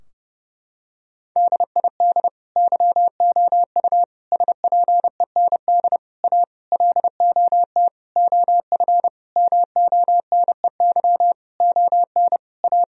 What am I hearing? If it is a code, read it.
Morse code, heard as a keyed tone, about 32 wpm, 700 Hz. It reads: DID YOU SPEND A LOT OF MONEY ON A
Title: Sentences from Top 1000 Words - Review Part 4 - 30wpm